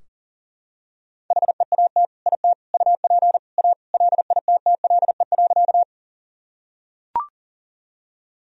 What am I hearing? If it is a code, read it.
Morse code, heard as a keyed tone, about 40 wpm, 700 Hz. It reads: HEAT IT UP A LITTLE.
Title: Sentences from Top 1000 Words - Review Part 3 - 40wpm